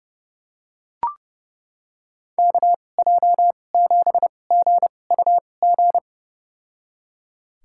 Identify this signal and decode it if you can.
Morse code, heard as a keyed tone, about 30 wpm, spaced as for 22 wpm, 700 Hz. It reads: KJ7GUG
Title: USA Call Signs - 6 Characters - 2x Character Spacing - 30wpm